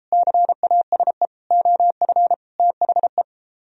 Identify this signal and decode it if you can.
Morse code, heard as a keyed tone, about 33 wpm, 700 Hz. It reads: CASE OF THE